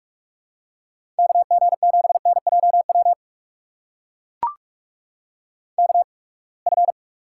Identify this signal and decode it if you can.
Morse code, heard as a keyed tone, about 43 wpm, spaced as for 30 wpm, 700 Hz. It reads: KG7NJW KF
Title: USA Call Signs - 6 Characters - 8x Character Spacing - 45wpm